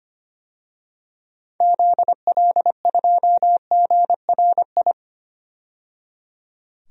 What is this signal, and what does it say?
Morse code, heard as a keyed tone, about 25 wpm, 700 Hz. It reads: ZL2GRI
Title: DXCC-100 Call Signs - 6 Characters - 25wpm